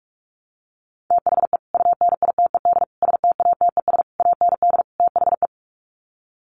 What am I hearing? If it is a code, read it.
Morse code, heard as a keyed tone, about 44 wpm, 700 Hz. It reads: THE UNITED STATES AND THE
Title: Sets of 5 Words 45wpm